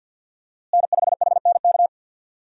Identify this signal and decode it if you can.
Morse code, heard as a keyed tone, about 50 wpm, 700 Hz. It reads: N5HNK
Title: USA Call Signs - 5 Characters - Repeated 2x - 50wpm